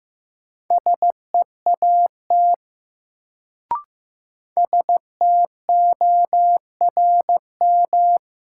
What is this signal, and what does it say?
Morse code, heard as a keyed tone, about 15 wpm, 700 Hz. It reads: SEAT STORM
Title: Sentences from Top 1500 Words - New Word Drill - 15wpm